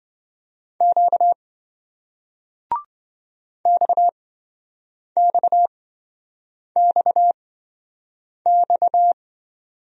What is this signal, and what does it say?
Morse code, heard as a keyed tone, about 27 wpm, spaced as for 7 wpm, 700 Hz. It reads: Q XXXX